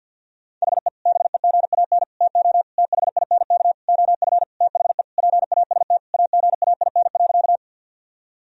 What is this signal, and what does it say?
Morse code, heard as a keyed tone, about 50 wpm, 700 Hz. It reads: HE BEGAN TO THINK OF THE PAST AGAIN.
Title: Sentences from Top 1000 Words - Review Part 3 - 50wpm